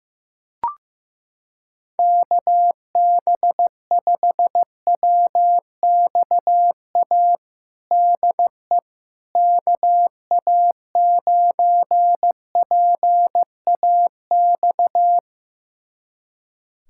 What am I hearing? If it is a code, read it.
Morse code, heard as a keyed tone, about 15 wpm, 700 Hz. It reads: KB5WXA DE KA9PAX
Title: USA Callsign DE Callsign - 6 Characters - 15wpm